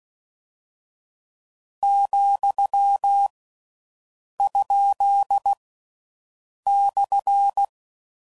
Morse code, heard as a keyed tone, about 16 wpm, 800 Hz. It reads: , ? /
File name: Mixed / Misc / Punct